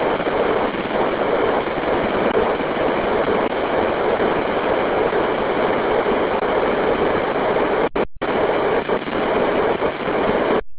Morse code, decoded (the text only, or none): none